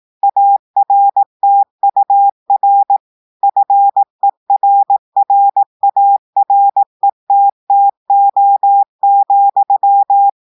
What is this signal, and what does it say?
Morse code, heard as a keyed tone, about 18 wpm, 800 Hz. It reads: ARTUR FERRARETTO,
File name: audios / portugais / portugais texte en clair 1 vitesse 18 mots minutes